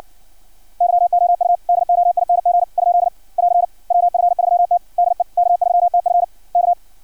Morse code, heard as a keyed tone, about 56 wpm, 700 Hz. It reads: YOU DOING ? <AR> WF3T DE K3TF K
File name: archived / CRSnet